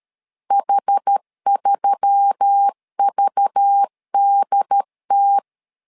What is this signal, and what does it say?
Morse code, heard as a keyed tone, about 13 wpm, 800 Hz. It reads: H3VDT
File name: LEZIONI / 5 LBVC / 5 LBV 12wpm 1855